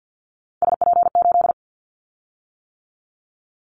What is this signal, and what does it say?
Morse code, heard as a keyed tone, about 50 wpm, 700 Hz. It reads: SP8
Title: Callsign Prefixes for Top 25 DXCC - 50wpm